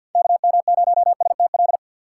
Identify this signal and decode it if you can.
Morse code, heard as a keyed tone, about 47 wpm, 700 Hz. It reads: KM0STL
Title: USA Call Signs - 6 Characters - 4x Character Spacing - 50wpm